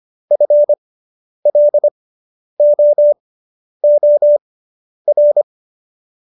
Morse code, heard as a keyed tone, about 25 wpm, spaced as for 10 wpm, 590 Hz. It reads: FLOOR